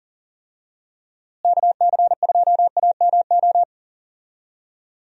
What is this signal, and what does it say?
Morse code, heard as a keyed tone, about 40 wpm, 700 Hz. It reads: KC2AMO
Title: USA Call Signs - 6 Characters - Repeated 2x - 40wpm